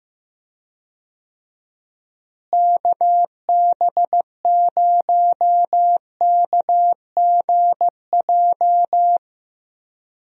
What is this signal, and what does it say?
Morse code, heard as a keyed tone, about 15 wpm, 700 Hz. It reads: KB0KGJ
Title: USA Call Signs - 6 Characters - Repeated 3x - 15wpm